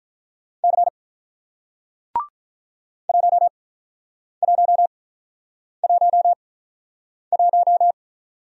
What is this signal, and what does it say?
Morse code, heard as a keyed tone, about 46 wpm, spaced as for 9 wpm, 700 Hz. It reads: C 1111